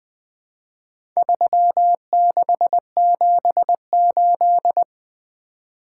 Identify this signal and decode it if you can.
Morse code, heard as a keyed tone, about 20 wpm, 700 Hz. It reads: 3678
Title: Four Letter-Number Combinations - 20wpm